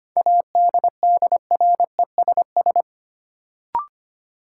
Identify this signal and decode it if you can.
Morse code, heard as a keyed tone, about 25 wpm, 700 Hz. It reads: ADDRESS
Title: Sentences from Top 1000 Words - New Word Drill - 25wpm